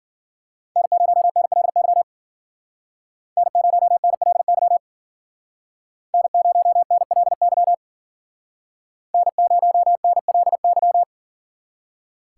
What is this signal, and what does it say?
Morse code, heard as a keyed tone, about 49 wpm, 700 Hz. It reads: N0NLY N0NLY N0NLY N0NLY
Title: Speed Racing - USA 1x3 Call Signs 40wpm